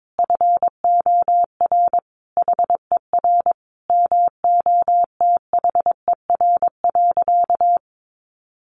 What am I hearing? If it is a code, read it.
Morse code, heard as a keyed tone, about 22 wpm, 700 Hz. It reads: FOR HER MOTHER.
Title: Sentences from Top 900 Words 22wpm